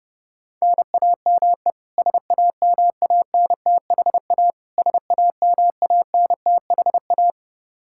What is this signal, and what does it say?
Morse code, heard as a keyed tone, about 32 wpm, 700 Hz. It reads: NAME SAMANTHA SAMANTHA
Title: Practice QSO - Exchange 1 - Beginning - 30wpm